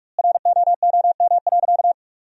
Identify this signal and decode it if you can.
Morse code, heard as a keyed tone, about 46 wpm, 700 Hz. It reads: ROOM.
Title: 2x Word-Spacing - Sentences from Top 400 Words - 45wpm